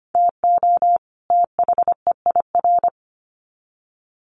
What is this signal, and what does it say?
Morse code, heard as a keyed tone, about 25 wpm, 700 Hz. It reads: TO THEIR
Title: Sets of 2 Words 25wpm